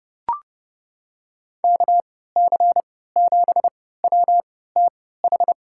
Morse code, heard as a keyed tone, about 31 wpm, spaced as for 17 wpm, 700 Hz. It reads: KC7WTH
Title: USA Call Signs - 6 Characters - 3x Character Spacing - 30wpm